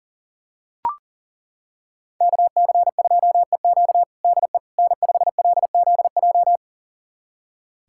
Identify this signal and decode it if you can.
Morse code, heard as a keyed tone, about 40 wpm, 690 Hz. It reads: KC2EQ DE N5LZJ
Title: USA Callsign DE Callsign - 5 Characters - 40wpm